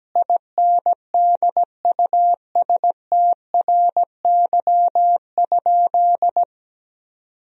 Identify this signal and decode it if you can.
Morse code, heard as a keyed tone, about 17 wpm, 700 Hz. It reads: INDUSTRY?
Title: Sentences from Top 1000 Words - Review Part 4 - 17wpm